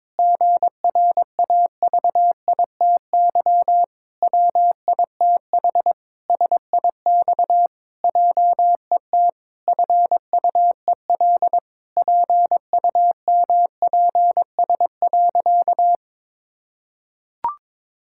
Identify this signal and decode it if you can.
Morse code, heard as a keyed tone, about 22 wpm, 700 Hz. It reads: GRAVITY WITH SIX JET FUEL PUMPS.
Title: Pangram Challenge v2 - 22wpm